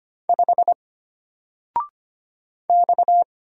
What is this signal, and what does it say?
Morse code, heard as a keyed tone, about 25 wpm, 700 Hz. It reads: 5 X
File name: Single-Letter-Number-25wpm